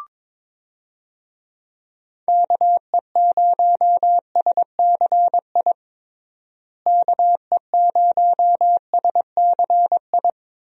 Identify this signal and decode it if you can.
Morse code, heard as a keyed tone, about 22 wpm, 700 Hz. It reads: KE0SCI KE0SCI
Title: USA Call Signs - 6 Characters - Repeated 2x - 22wpm